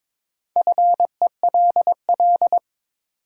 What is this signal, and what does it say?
Morse code, heard as a keyed tone, about 22 wpm, 700 Hz. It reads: FELL
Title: Sets of 5 Words Encore - New Word Drill - 22wpm